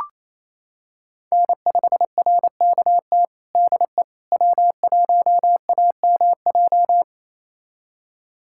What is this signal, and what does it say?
Morse code, heard as a keyed tone, about 28 wpm, 700 Hz. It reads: N5RKT DE W1AMJ
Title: USA Callsign DE Callsign - 5 Characters - 28wpm